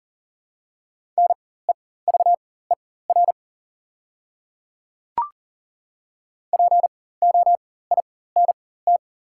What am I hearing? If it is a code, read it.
Morse code, heard as a keyed tone, about 40 wpm, spaced as for 19 wpm, 700 Hz. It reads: NEVER POINT
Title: 4x Character Spacing - Top 300 Words - 40wpm